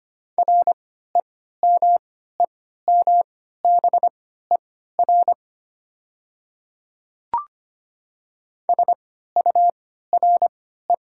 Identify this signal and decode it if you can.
Morse code, heard as a keyed tone, about 25 wpm, spaced as for 14 wpm, 690 Hz. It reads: REMEMBER SURE
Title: 3x Character Spacing - Top 500 Words - 25wpm